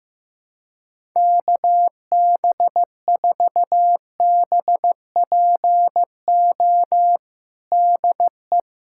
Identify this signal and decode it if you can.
Morse code, heard as a keyed tone, about 15 wpm, 700 Hz. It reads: KB4BPO DE
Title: USA Callsign DE Callsign - 6 Characters - 15wpm